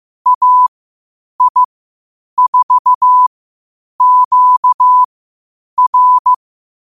Morse code, heard as a keyed tone, about 15 wpm, spaced as for 8 wpm, 1000 Hz. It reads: AI4QR
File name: qotdfiles / 8wpm